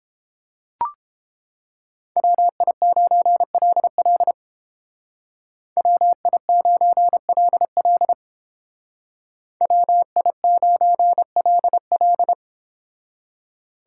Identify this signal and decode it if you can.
Morse code, heard as a keyed tone, about 29 wpm, 700 Hz. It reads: WI9LL WI9LL WI9LL